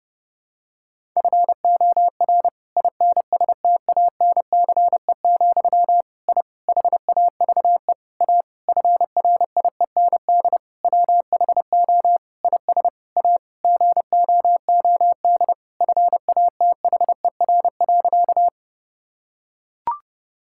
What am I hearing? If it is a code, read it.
Morse code, heard as a keyed tone, about 30 wpm, 700 Hz. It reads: FOR INSTANCE, I HAVE A FRIEND WHO IS A GOOD FATHER.
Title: Sentences from Top 1000 Words - Review Part 5 - 30wpm